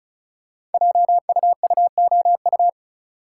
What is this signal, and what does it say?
Morse code, heard as a keyed tone, about 36 wpm, 690 Hz. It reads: JUUOU
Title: Code Groups - 5 Letters - 35wpm